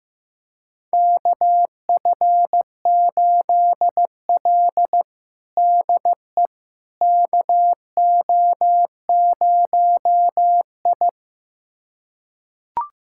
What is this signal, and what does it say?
Morse code, heard as a keyed tone, about 15 wpm, 700 Hz. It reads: KF8L DE KO0I
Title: USA Callsign DE Callsign - 4 Characters - 15wpm